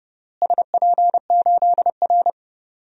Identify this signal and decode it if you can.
Morse code, heard as a keyed tone, about 30 wpm, 700 Hz. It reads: SP8R
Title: DXCC-100 Call Signs - 4 Characters - Repeated 3x - 30wpm